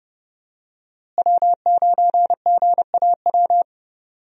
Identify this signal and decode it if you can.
Morse code, heard as a keyed tone, about 30 wpm, 700 Hz. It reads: W9GAW
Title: USA Call Signs - 5 Characters - Repeated 2x - 30wpm